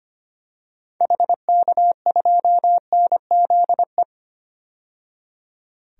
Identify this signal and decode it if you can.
Morse code, heard as a keyed tone, about 25 wpm, 700 Hz. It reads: HK2NZE
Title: DXCC-100 Call Signs - 6 Characters - 25wpm